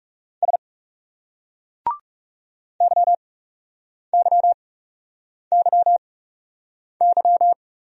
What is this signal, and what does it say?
Morse code, heard as a keyed tone, about 38 wpm, spaced as for 8 wpm, 700 Hz. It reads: S YYYY
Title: Speed-Racing Single Letter-Number 30wpm